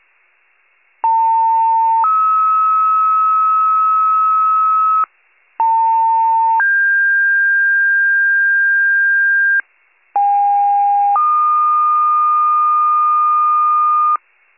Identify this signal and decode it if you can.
Two-tone page, heard as a one-second tone, then a three-second tone, about 900 then 1290 Hz, about 900 then 1600 Hz, about 810 then 1220 Hz. It is heard from 1.0 to 14.2 s.
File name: pager / 78 RESCUE DUTY / wcalert